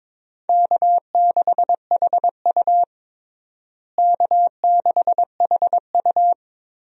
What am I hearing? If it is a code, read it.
Morse code, heard as a keyed tone, about 22 wpm, 700 Hz. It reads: K6HU K6HU
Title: USA Call Signs - 4 Characters - Repeated 3x - 22wpm